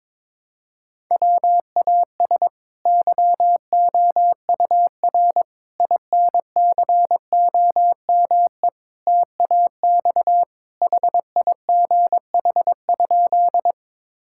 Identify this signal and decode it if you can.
Morse code, heard as a keyed tone, about 21 wpm, 700 Hz. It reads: WAS YOUR INCOME TAX HIGH?
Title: Sentences from Top 1000 Words - Review Part 5 - 22wpm